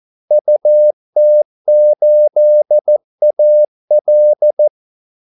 Morse code, heard as a keyed tone, about 14 wpm, 600 Hz. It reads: UT8AL
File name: Callsigns-14wpm